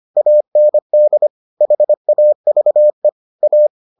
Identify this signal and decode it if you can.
Morse code, heard as a keyed tone, about 25 wpm, 600 Hz. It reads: AND HAVE A